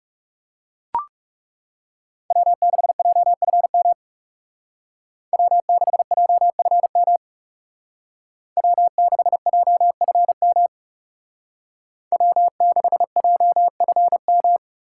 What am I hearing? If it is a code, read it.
Morse code, heard as a keyed tone, about 37 wpm, 700 Hz. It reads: W6JFM W6JFM W6JFM W6JFM
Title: Speed Racing - USA 1x3 Call Signs 30wpm